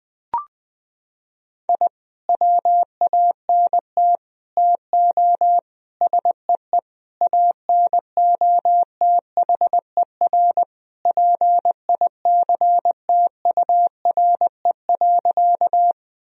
Morse code, heard as a keyed tone, about 20 wpm, 700 Hz. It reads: I WANT TO SEE ANOTHER PICTURE.
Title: Sentences from Top 1000 Words - Review Part 1 - 20wpm